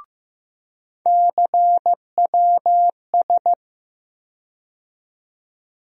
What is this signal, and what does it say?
Morse code, heard as a keyed tone, about 15 wpm, 700 Hz. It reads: CWS
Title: USA Call Sign Postfixes - 3 Letters 15wpm